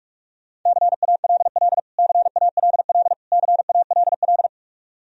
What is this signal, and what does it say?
Morse code, heard as a keyed tone, about 46 wpm, 700 Hz. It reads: CALL CALL CALL
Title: Top 100 Words Repeated 45wpm